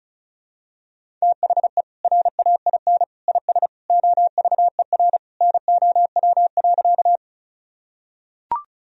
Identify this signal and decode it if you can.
Morse code, heard as a keyed tone, about 36 wpm, 700 Hz. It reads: THE RAIN IS OVER NOW.
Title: Sentences from Top 1000 Words - Review Part 1 - 35wpm